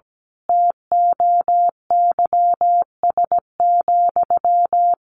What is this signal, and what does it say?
Morse code, heard as a keyed tone, about 17 wpm, 700 Hz. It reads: TOYS,
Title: A Sound Of Thunder 17wpm